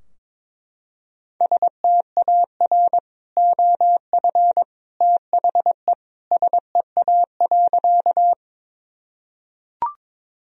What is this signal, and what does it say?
Morse code, heard as a keyed tone, about 22 wpm, 700 Hz. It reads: STAR OF THE SEA.
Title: Sentences from Top 1000 Words - Review Part 2 - 22wpm